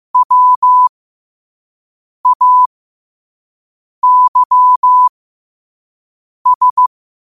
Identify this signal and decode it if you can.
Morse code, heard as a keyed tone, about 15 wpm, spaced as for 5 wpm, 1000 Hz. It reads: WAYS
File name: qotdfiles / 5wpm